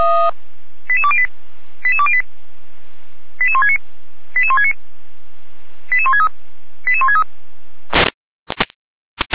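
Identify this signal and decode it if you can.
Five-tone selective call (ZVEI), heard as quick runs of five tones, tones of about 70 ms. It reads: 88208 88208 88169 88169 88163 88163
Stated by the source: ZVEI five-tone call received on SDR.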